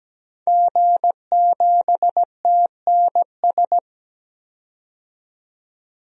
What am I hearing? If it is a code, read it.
Morse code, heard as a keyed tone, about 17 wpm, 700 Hz. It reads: G7TNS
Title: DXCC-100 Call Signs - 5 Characters - 17wpm